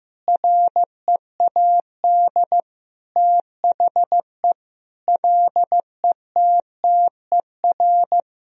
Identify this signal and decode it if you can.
Morse code, heard as a keyed tone, about 15 wpm, 700 Hz. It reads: READ THE LETTER
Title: Sentences from Top 300 Words and Abbreviations 15wpm